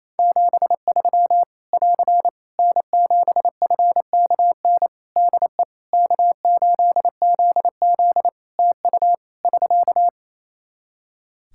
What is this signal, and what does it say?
Morse code, heard as a keyed tone, about 27 wpm, 700 Hz. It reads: 73 <AR> N7FKN DE K8ZZ TU <SK>
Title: Practice QSO - Exchange 4 - Ending - 28wpm